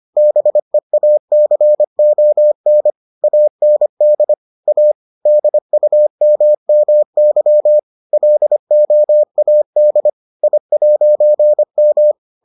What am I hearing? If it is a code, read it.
Morse code, heard as a keyed tone, about 25 wpm, 600 Hz. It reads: BEACON AND A DUMMY LOAD I'M